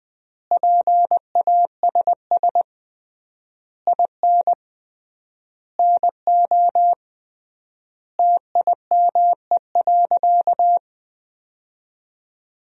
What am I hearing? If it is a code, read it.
Morse code, heard as a keyed tone, about 20 wpm, 700 Hz. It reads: PASS IN NO TIME.
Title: 3x Word-Spacing - Sentences from Top 400 Words - 20wpm